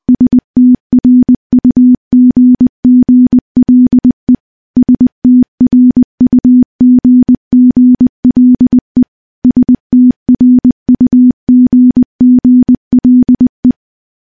Morse code, heard as a keyed tone, about 20 wpm, 250 Hz. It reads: STRUGGLE STRUGGLE STRUGGLE
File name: morse-training-20WPM